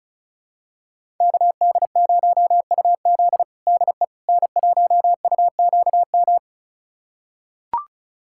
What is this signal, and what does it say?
Morse code, heard as a keyed tone, about 34 wpm, 700 Hz. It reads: KD0UZ DE N1UQM
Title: USA Callsign DE Callsign - 5 Characters - 35wpm